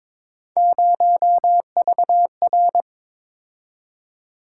Morse code, heard as a keyed tone, about 22 wpm, 700 Hz. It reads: 0VR
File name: practice / 22wpm / Three Letter-Number Combinations - 22wpm